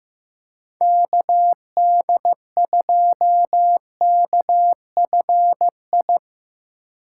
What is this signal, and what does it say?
Morse code, heard as a keyed tone, about 17 wpm, 700 Hz. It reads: KD2KFI
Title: Speed Racing - USA Call Signs 15wpm